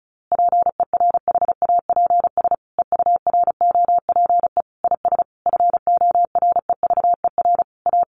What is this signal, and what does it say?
Morse code, heard as a keyed tone, about 34 wpm, 700 Hz. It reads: PERHAPS EUROPE IS FOREVER A
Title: A Sound Of Thunder 35wpm